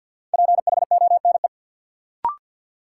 Morse code, heard as a keyed tone, about 50 wpm, 700 Hz. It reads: PHONE
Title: Sentences from Top 1100 Words - New Word Drill - 50wpm